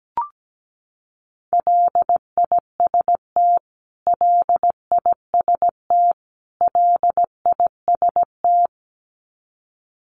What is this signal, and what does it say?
Morse code, heard as a keyed tone, about 17 wpm, 700 Hz. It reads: LIST LIST LIST
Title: Top 500 Words Repeated 17wpm